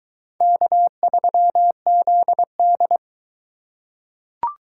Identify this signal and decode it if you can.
Morse code, heard as a keyed tone, about 20 wpm, 700 Hz. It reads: K3ZD